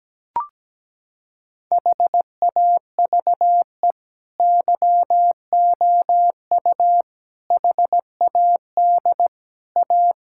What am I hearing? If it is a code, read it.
Morse code, heard as a keyed tone, about 17 wpm, 700 Hz. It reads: HAVE YOU HAD A
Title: Sentences from Top 1000 Words - Review Part 6 - 17wpm